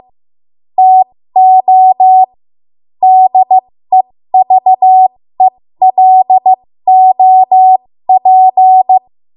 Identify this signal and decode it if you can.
Morse code, heard as a keyed tone, about 15 wpm, spaced as for 13 wpm, 750 Hz. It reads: TO DEVELOP
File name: Morse / 13WPM